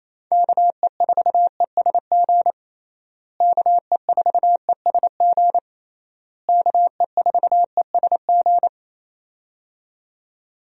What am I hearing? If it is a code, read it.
Morse code, heard as a keyed tone, about 28 wpm, 700 Hz. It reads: KE4ESG KE4ESG KE4ESG
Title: USA Call Signs - 6 Characters - Repeated 3x - 28wpm